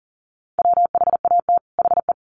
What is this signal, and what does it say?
Morse code, heard as a keyed tone, about 40 wpm, 700 Hz. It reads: WHAT HE
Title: Sets of 2 Words 40wpm